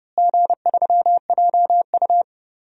Morse code, heard as a keyed tone, about 30 wpm, 700 Hz. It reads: G3JU
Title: DXCC-100 Call Signs - 4 Characters - Repeated 3x - 30wpm